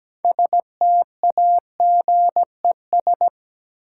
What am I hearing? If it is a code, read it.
Morse code, heard as a keyed tone, about 17 wpm, 700 Hz. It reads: STAGES
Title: Sets of 5 Words Encore - New Word Drill - 17wpm